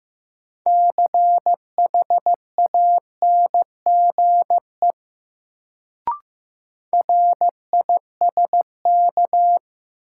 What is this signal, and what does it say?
Morse code, heard as a keyed tone, about 15 wpm, 700 Hz. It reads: CHANGE RISK
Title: Sentences from Top 600 Words - New Word Drill - 15wpm